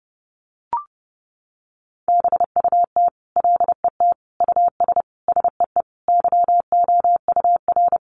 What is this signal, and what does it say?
Morse code, heard as a keyed tone, about 30 wpm, 700 Hz. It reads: BUT LET US SEE YOUR
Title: Sentences from Top 500 Words 30wpm